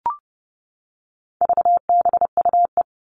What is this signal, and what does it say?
Morse code, heard as a keyed tone, about 30 wpm, 700 Hz. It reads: VBUE
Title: All Cut Numbers - 4-Digit 30wpm